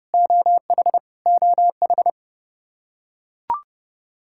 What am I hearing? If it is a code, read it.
Morse code, heard as a keyed tone, about 32 wpm, 700 Hz. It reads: OH OH
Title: Practice QSO - Exchange 1 - Beginning - 30wpm